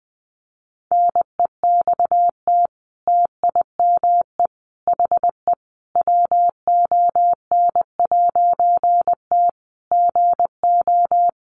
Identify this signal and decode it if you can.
Morse code, heard as a keyed tone, about 20 wpm, 700 Hz. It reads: NEXT TIME HE WON'T GO
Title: A Sound Of Thunder 20wpm